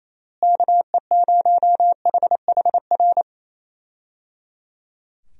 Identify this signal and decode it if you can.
Morse code, heard as a keyed tone, about 27 wpm, 700 Hz. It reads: KE0HHR
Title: USA Call Signs - 6 Characters - Repeated 3x - 28wpm